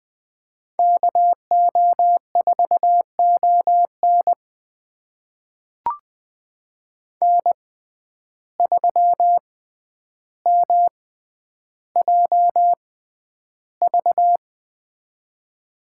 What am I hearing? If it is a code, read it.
Morse code, heard as a keyed tone, about 20 wpm, spaced as for 10 wpm, 700 Hz. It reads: KO4ON N3MJV
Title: USA Call Signs - 5 Characters - 6x Character Spacing - 20wpm